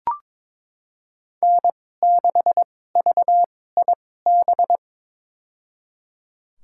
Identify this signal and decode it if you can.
Morse code, heard as a keyed tone, about 22 wpm, spaced as for 16 wpm, 700 Hz. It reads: N6VIB
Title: USA Call Signs - 5 Characters - 2x Character Spacing - 22wpm